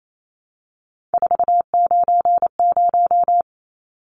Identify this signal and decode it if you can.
Morse code, heard as a keyed tone, about 28 wpm, 700 Hz. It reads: 490
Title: Numbers - 3-Digit 28wpm